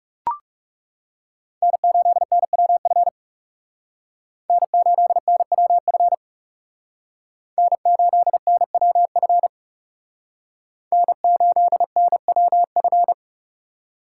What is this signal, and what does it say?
Morse code, heard as a keyed tone, about 38 wpm, 700 Hz. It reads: N8NWF N8NWF N8NWF N8NWF